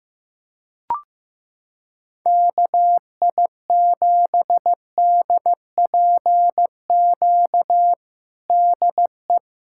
Morse code, heard as a keyed tone, about 15 wpm, 700 Hz. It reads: KI7DPQ DE
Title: USA Callsign DE Callsign - 6 Characters - 15wpm